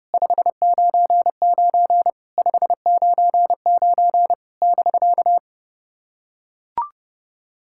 Morse code, heard as a keyed tone, about 30 wpm, 700 Hz. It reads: 599 599 <BK>
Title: SOTA QSO Practice 30wpm